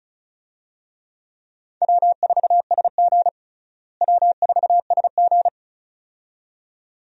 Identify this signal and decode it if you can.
Morse code, heard as a keyed tone, about 36 wpm, 700 Hz. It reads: W4SG W4SG
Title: USA Call Signs - 4 Characters - Repeated 2x - 35wpm